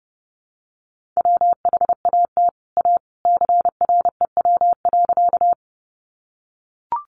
Morse code, heard as a keyed tone, about 30 wpm, 700 Hz. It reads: WHAT A CREW.
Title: Sentences from Top 1500 Words 30wpm